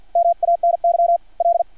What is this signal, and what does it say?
Morse code, heard as a keyed tone, about 50 wpm, 680 Hz. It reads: MANY L
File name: archived / CRSnet